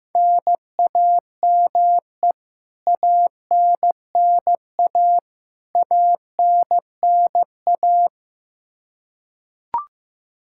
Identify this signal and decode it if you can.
Morse code, heard as a keyed tone, about 15 wpm, 700 Hz. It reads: NAME ANNA ANNA
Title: Practice QSO - Exchange 1 - Beginning - 15wpm